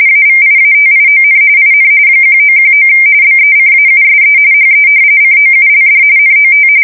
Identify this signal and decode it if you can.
RTTY, about 45.45 baud. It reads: AVIATION IN 1919 BECAUSE HE COULD NOT
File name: fsk170 / usnavy4